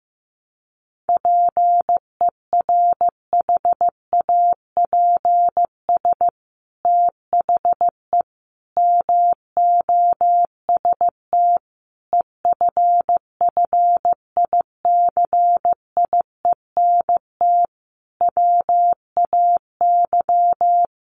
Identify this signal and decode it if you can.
Morse code, heard as a keyed tone, about 15 wpm, 700 Hz. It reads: PERHAPS THE MOST EFFICIENT WAY
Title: Sentences from Top 1300 Words 15wpm